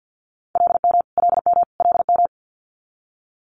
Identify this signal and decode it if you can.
Morse code, heard as a keyed tone, about 50 wpm, 690 Hz. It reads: LM LM LM
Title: Multi-letter Phonemes Repeated 3x 50wpm